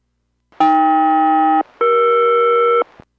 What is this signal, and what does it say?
Aircraft SELCAL, heard as two pairs of tones. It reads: AK-ER